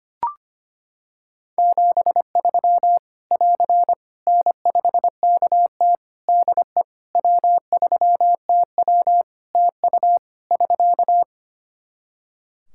Morse code, heard as a keyed tone, about 25 wpm, 700 Hz. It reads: 73 <AR> N5KT DE W3TW TU <SK>
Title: Practice QSO - Exchange 4 - Ending - 25wpm